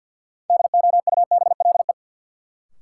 Morse code, heard as a keyed tone, about 46 wpm, 700 Hz. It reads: DOUBLE
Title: Sentences from Top 1400 Words - New Word Drill Alt - 50wpm